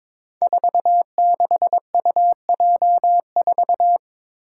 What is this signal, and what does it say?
Morse code, heard as a keyed tone, about 23 wpm, 700 Hz. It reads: 46UJ4
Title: Five Challenging Letter-Number Combinations 22wpm